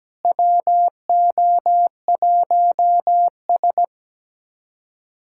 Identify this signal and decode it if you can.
Morse code, heard as a keyed tone, about 17 wpm, 700 Hz. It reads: WO1S
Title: USA Callsign DE Callsign - 4 Characters - 17wpm